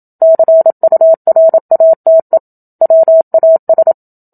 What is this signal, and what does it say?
Morse code, heard as a keyed tone, about 27 wpm, 650 Hz. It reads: CURATE WAS
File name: Book Two / 01-Chapter 1 - 27 WPM